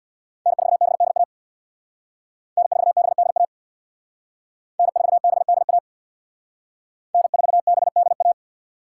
Morse code, heard as a keyed tone, about 62 wpm, 700 Hz. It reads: N4BDA N4BDA N4BDA N4BDA
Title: Speed Racing - USA 1x3 Call Signs 50wpm